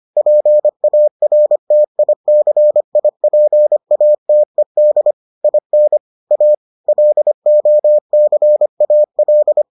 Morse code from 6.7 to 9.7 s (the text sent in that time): LOCAL